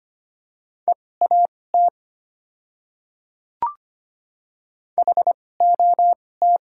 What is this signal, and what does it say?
Morse code, heard as a keyed tone, about 25 wpm, spaced as for 18 wpm, 700 Hz. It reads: EAT HOT
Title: 2x Character Spacing - Top 500 Words - 25wpm